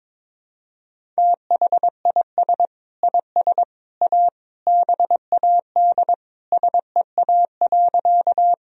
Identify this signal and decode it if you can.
Morse code, heard as a keyed tone, about 22 wpm, 700 Hz. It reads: THIS IS A BAD SEA.
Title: Sentences from Top 1000 Words - Review Part 1 - 22wpm